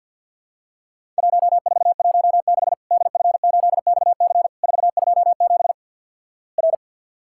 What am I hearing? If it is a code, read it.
Morse code, heard as a keyed tone, about 54 wpm, 700 Hz. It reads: 1416 DF8XC 427 R
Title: Worked All Europe DX Contest - QTC Exchange - 50wpm